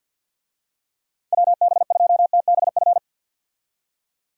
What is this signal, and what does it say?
Morse code, heard as a keyed tone, about 50 wpm, 700 Hz. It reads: WB2TBF
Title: USA Call Signs - 6 Characters - 50wpm